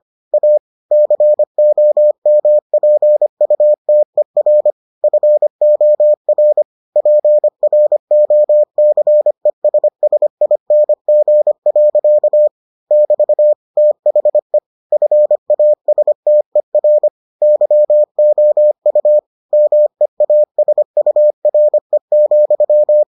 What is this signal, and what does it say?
Morse code, heard as a keyed tone, about 25 wpm, 600 Hz. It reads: A COMPUTER FOR PROCESSING. <BT> THE FASTER YOU MEASURE,